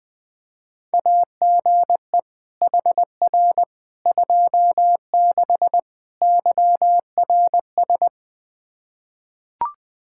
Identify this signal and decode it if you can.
Morse code, heard as a keyed tone, about 20 wpm, 700 Hz. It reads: AGE HR 26 YRS
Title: Practice QSO - Exchange 3 - More Info - 20wpm